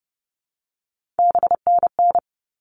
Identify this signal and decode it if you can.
Morse code, heard as a keyed tone, about 30 wpm, 700 Hz. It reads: BNN
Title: All Cut Numbers - 3-Digit 30wpm